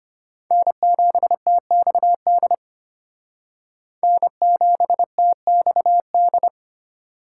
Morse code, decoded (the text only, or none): N7TXD N7TXD